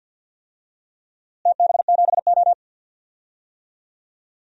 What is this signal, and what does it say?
Morse code, heard as a keyed tone, about 52 wpm, 700 Hz. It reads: TB7O
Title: DXCC-100 Call Signs - 4 Characters - Repeated 3x - 50wpm